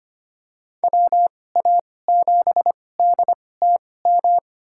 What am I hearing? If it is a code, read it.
Morse code, heard as a keyed tone, about 25 wpm, spaced as for 18 wpm, 700 Hz. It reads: WA7DTM
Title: USA Call Signs - 6 Characters - 2x Character Spacing - 25wpm